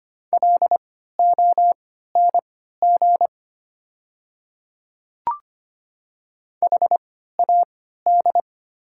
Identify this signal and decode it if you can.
Morse code, heard as a keyed tone, about 25 wpm, spaced as for 14 wpm, 700 Hz. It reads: LONG HAD
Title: 3x Character Spacing - Top 100 Words - 25wpm